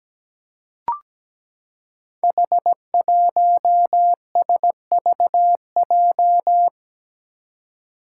Morse code, heard as a keyed tone, about 17 wpm, 700 Hz. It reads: H1SVJ
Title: Five Challenging Letter-Number Combinations 17wpm